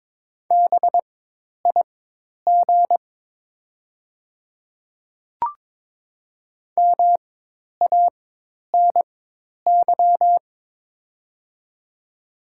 Morse code, heard as a keyed tone, about 22 wpm, spaced as for 10 wpm, 700 Hz. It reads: BIG MANY
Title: 4x Character Spacing - Top 200 Words - 22wpm